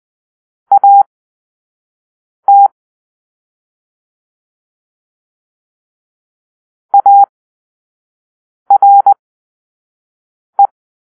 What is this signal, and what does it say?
Morse code, heard as a keyed tone, about 20 wpm, spaced as for 5 wpm, 800 Hz. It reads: AT ARE
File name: arrl / Grp-1 Drill-1